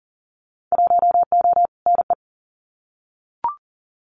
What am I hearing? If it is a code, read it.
Morse code, heard as a keyed tone, about 40 wpm, 700 Hz. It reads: 1O NE
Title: Winter Field Day Exchanges 40wpm